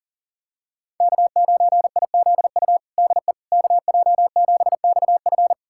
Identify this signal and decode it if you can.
Morse code, heard as a keyed tone, about 37 wpm, 700 Hz. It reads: K9IZU DE KJ7XF
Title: USA Callsign DE Callsign - 5 Characters - 40wpm